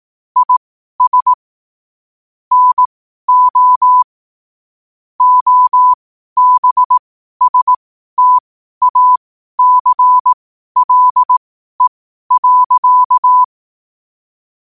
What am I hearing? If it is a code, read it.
Morse code, heard as a keyed tone, about 18 wpm, spaced as for 12 wpm, 1000 Hz. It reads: IS NO OBSTACLE.
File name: qotdfiles / 12wpm